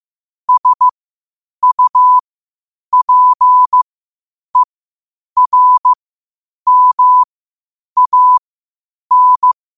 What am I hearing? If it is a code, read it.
Morse code, heard as a keyed tone, about 15 wpm, spaced as for 8 wpm, 1000 Hz. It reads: SUPERMAN
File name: qotdfiles / 8wpm